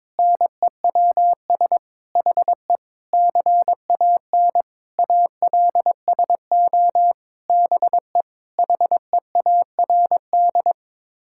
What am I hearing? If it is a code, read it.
Morse code, heard as a keyed tone, about 21 wpm, 700 Hz. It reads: NEWS HE CAN ALSO BE HEARD
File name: Sets of 9 Words - 22wpm